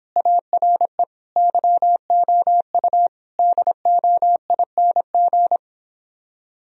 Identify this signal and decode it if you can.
Morse code, heard as a keyed tone, about 26 wpm, 700 Hz. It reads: ARE YOU DOING
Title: Speed-Racing - Sets of 4 Words 22wpm